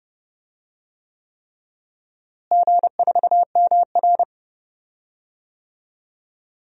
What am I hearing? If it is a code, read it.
Morse code, heard as a keyed tone, about 32 wpm, 700 Hz. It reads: G4MR
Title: DXCC-100 Call Signs - 4 Characters - Repeated 3x - 30wpm